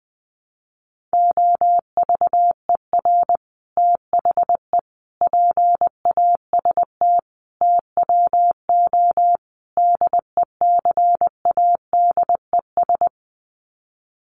Morse code, heard as a keyed tone, about 20 wpm, 700 Hz. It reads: OVER THE PAST TWO DECADES
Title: Sets of 5 Words Encore2 20wpm